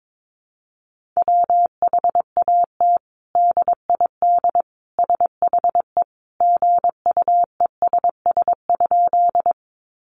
Morse code, heard as a keyed tone, about 22 wpm, 700 Hz. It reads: WHAT DID SHE GUESS?
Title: Sentences from Top 1500 Words 22wpm